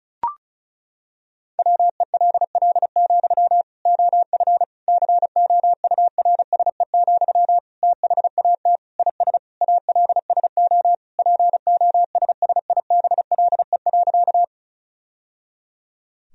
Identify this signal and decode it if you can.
Morse code, heard as a keyed tone, about 34 wpm, 700 Hz. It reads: WELL, OF COURSE, THAT IS ALSO POSSIBLE.
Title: Sentences from Top 1000 Words - Review Part 1 - 35wpm